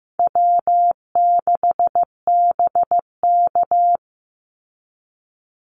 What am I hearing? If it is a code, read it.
Morse code, heard as a keyed tone, about 15 wpm, 700 Hz. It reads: W6BK
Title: Call Signs Repeated 3x - 15wpm